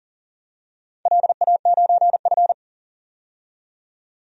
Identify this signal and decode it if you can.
Morse code, heard as a keyed tone, about 38 wpm, 700 Hz. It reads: LA9F
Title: DXCC-100 Call Signs - 4 Characters - 40wpm